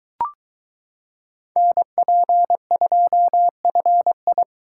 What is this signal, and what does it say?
Morse code, heard as a keyed tone, about 22 wpm, 700 Hz. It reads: NP2FI